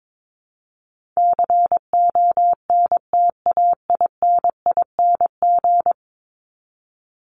Morse code, heard as a keyed tone, about 22 wpm, 700 Hz. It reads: CONTAINING
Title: ING Suffix - Top 100 Words with Base Words 22wpm